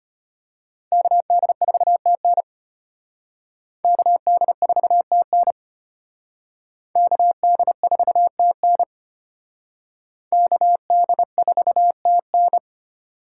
Morse code, heard as a keyed tone, about 34 wpm, 700 Hz. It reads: KD4TN KD4TN KD4TN KD4TN